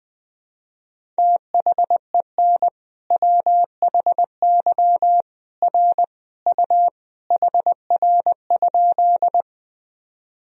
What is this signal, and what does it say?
Morse code, heard as a keyed tone, about 19 wpm, 700 Hz. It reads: THEN WHY R U HR?
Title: Sentences from Top 200 Words and Abbreviations 20wpm